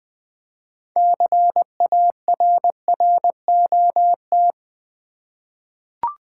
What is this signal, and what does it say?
Morse code, heard as a keyed tone, about 20 wpm, 700 Hz. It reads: CARROT
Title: 6 Letter Words - First 3 Letters Given - 20wpm